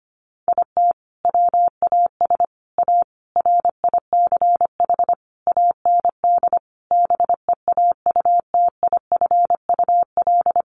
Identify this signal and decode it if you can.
Morse code, heard as a keyed tone, about 25 wpm, 700 Hz. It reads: IT WAS A RICH AND BEAUTIFUL
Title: Sentences from Top 1100 Words 25wpm